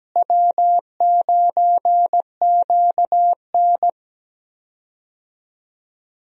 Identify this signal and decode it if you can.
Morse code, heard as a keyed tone, about 17 wpm, 700 Hz. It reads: W9QN
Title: USA Call Signs - 4 Characters - Repeated 2x - 17wpm